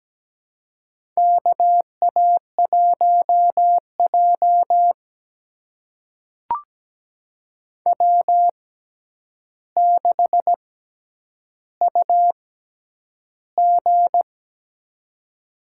Morse code, heard as a keyed tone, about 17 wpm, spaced as for 9 wpm, 700 Hz. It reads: KA1J W6UG
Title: USA Call Signs - 4 Characters - 6x Character Spacing - 17wpm